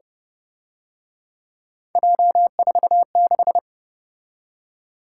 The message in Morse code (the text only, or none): J46